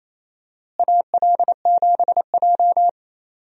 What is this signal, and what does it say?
Morse code, heard as a keyed tone, about 28 wpm, 700 Hz. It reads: AL7J
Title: USA Call Signs - 4 Characters - Repeated 2x - 28wpm